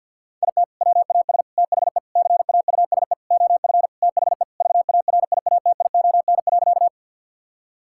Morse code, heard as a keyed tone, about 50 wpm, 700 Hz. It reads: IT WAS THE CAUSE OF THE VARIATION.
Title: Sentences from Top 1600 Words 50wpm